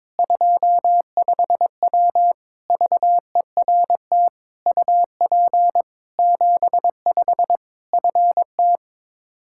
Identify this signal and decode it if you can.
Morse code, heard as a keyed tone, about 22 wpm, 700 Hz. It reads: 25W VERT UP 75 FT
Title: Practice QSO - Exchange 2 - Info - 22wpm